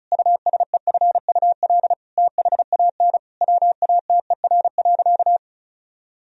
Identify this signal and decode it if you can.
Morse code, heard as a keyed tone, about 36 wpm, 700 Hz. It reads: USEFUL THAN WATER.
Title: Sentences from Top 1000 Words - Review Part 4 - 35wpm